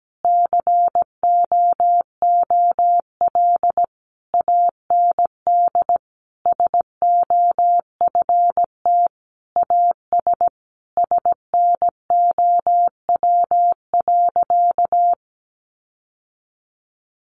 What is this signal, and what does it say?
Morse code, heard as a keyed tone, about 17 wpm, 700 Hz. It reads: COOL AND SOFT AS SNOW.
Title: Sentences from Top 1500 Words 17wpm